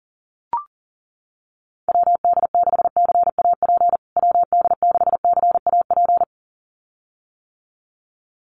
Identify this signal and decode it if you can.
Morse code, heard as a keyed tone, about 40 wpm, 700 Hz. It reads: WD6CAP WD6CAP
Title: USA Call Signs Repeated 2x 40wpm